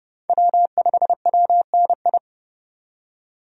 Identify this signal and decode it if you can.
Morse code, heard as a keyed tone, about 29 wpm, 700 Hz. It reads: W5WNI
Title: USA Call Signs - 5 Characters - 30wpm